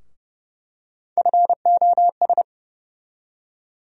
Morse code, heard as a keyed tone, about 32 wpm, 700 Hz. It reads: FOS
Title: USA Call Sign Postfixes - 3 Letters 30wpm